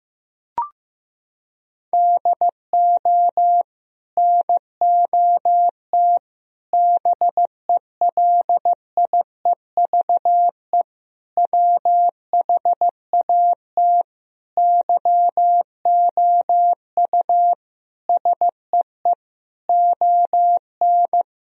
Morse code, heard as a keyed tone, about 15 wpm, 700 Hz. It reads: DO NOT BELIEVE WHAT YOU SEE ON
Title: Sentences from Top 1000 Words - Review Part 4 - 15wpm